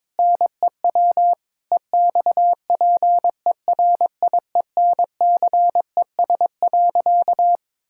Morse code, heard as a keyed tone, about 22 wpm, 700 Hz. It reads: NEW EXPERIENCES.
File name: Difficult Sentences with Hints - 22wpm